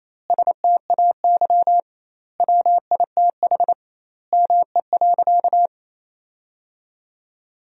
Morse code, heard as a keyed tone, about 28 wpm, 700 Hz. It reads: STAY WITH ME.
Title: 2x Word-Spacing - Sentences from Top 500 Words - 28wpm